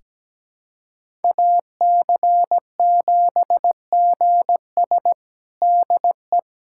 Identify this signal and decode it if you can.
Morse code, heard as a keyed tone, about 17 wpm, 700 Hz. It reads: AC7GS DE
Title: USA Callsign DE Callsign - 5 Characters - 17wpm